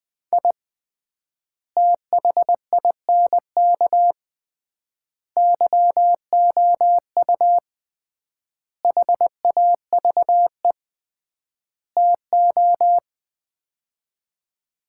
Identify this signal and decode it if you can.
Morse code, heard as a keyed tone, about 20 wpm, 700 Hz. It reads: I THINK YOU HAVE TO
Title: 3x Word-Spacing - Sets of 5 Words - 20wpm